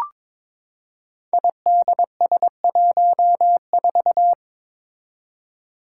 Morse code, heard as a keyed tone, about 22 wpm, 700 Hz. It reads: IDS14
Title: Five Challenging Letter-Number Combinations 22wpm